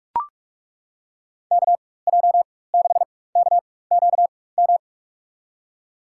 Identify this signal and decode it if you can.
Morse code, heard as a keyed tone, about 42 wpm, spaced as for 21 wpm, 700 Hz. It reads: KJ6KQM
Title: USA Call Signs - 4x Character Spacing - 45wpm